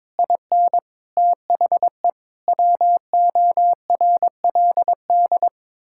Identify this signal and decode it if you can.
Morse code, heard as a keyed tone, about 22 wpm, 700 Hz. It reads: IN THE WORLD
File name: practice / 22wpm / Sets of 6 Words - 22wpm